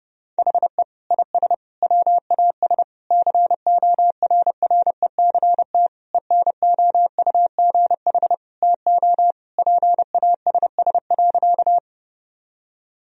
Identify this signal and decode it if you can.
Morse code, heard as a keyed tone, about 32 wpm, 700 Hz. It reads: HE IS WAS CORRECT ENOUGH TO PASS.
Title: Sentences from Top 1000 Words - Review Part 2 - 30wpm